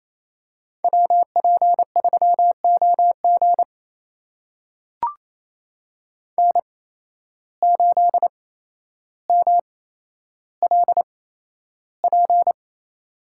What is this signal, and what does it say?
Morse code, heard as a keyed tone, about 28 wpm, spaced as for 12 wpm, 700 Hz. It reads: WP3OG N8MLP
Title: USA Call Signs - 5 Characters - 8x Character Spacing - 28wpm